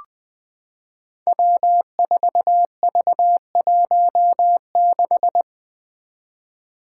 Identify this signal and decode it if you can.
Morse code, heard as a keyed tone, about 20 wpm, 700 Hz. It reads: W4V16
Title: Five Challenging Letter-Number Combinations 20wpm